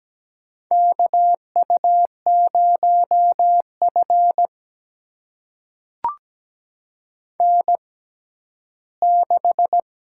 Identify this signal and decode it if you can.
Morse code, heard as a keyed tone, about 17 wpm, spaced as for 11 wpm, 700 Hz. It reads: KU0F N6
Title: USA Call Signs - 4 Characters - 6x Character Spacing - 17wpm